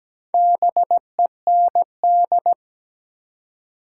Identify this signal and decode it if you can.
Morse code, heard as a keyed tone, about 17 wpm, 700 Hz. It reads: BEND
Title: Four Letter Words - 17wpm